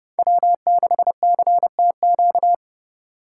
Morse code, heard as a keyed tone, about 32 wpm, 700 Hz. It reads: W6CTQ
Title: USA Call Signs - 5 Characters - 2x Character Spacing - 30wpm